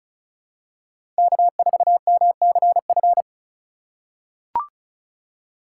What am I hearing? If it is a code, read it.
Morse code, heard as a keyed tone, about 34 wpm, 700 Hz. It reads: K4MCF
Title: USA Call Signs - 5 Characters - 3x Character Spacing - 35wpm